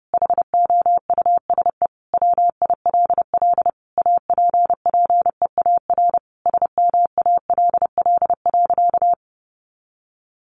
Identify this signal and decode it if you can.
Morse code, heard as a keyed tone, about 30 wpm, 700 Hz. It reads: HOUSE WILL APPEAR SMALL.
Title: Sentences from Top 1500 Words 30wpm